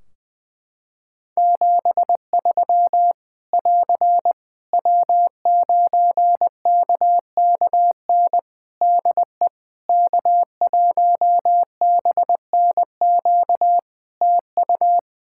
Morse code, heard as a keyed tone, about 20 wpm, 700 Hz. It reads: 73 <AR> W9KKN DE K1BNQ TU
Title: K1USN SST QSO Practice 20wpm